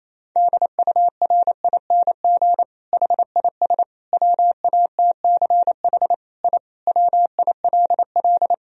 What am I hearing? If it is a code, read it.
Morse code, heard as a keyed tone, about 28 wpm, 700 Hz. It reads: DURING HIS WATCH I WILL
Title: Sentences from Top 1000 Words - Review Part 2 - 28wpm